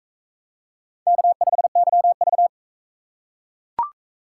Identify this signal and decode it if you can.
Morse code, heard as a keyed tone, about 41 wpm, 700 Hz. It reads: K5YV